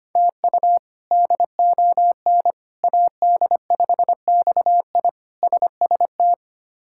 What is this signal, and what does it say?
Morse code, heard as a keyed tone, about 25 wpm, 700 Hz. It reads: TU DON AD5XI SST
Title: K1USN SST QSO Practice - 25wpm